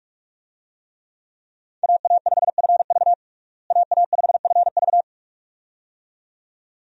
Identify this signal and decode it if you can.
Morse code, heard as a keyed tone, about 46 wpm, 700 Hz. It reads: AA5FV AA5FV
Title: USA Call Signs - 5 Characters - Repeated 2x - 45wpm